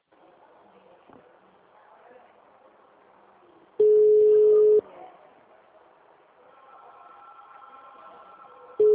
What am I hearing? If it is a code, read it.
Ringback tone (Europe, 425 Hz).
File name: out- -2003